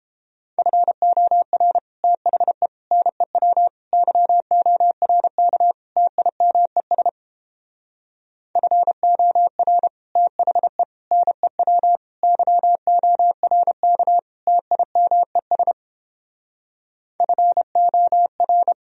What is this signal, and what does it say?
Morse code, heard as a keyed tone, about 29 wpm, 700 Hz. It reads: FOR THE NEW YORK TIMES FOR THE NEW YORK TIMES FOR